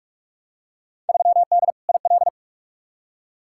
Morse code, heard as a keyed tone, about 47 wpm, 700 Hz. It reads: 3D IL
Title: Field Day Exchanges Encore - 45wpm